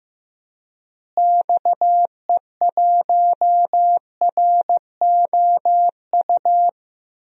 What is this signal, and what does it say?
Morse code, heard as a keyed tone, about 14 wpm, 700 Hz. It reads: XE1ROU
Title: DXCC-100 Call Signs - 6 Characters - Repeated 3x - 15wpm